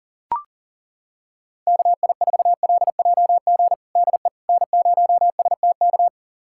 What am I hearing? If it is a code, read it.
Morse code, heard as a keyed tone, about 40 wpm, 700 Hz. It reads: KI4LJG DE N0STK
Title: USA Callsign DE Callsign 40wpm